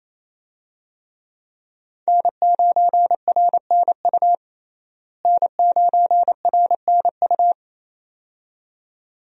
Morse code, heard as a keyed tone, about 28 wpm, 700 Hz. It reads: N9RNU N9RNU
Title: USA Call Signs - 5 Characters - Repeated 2x - 28wpm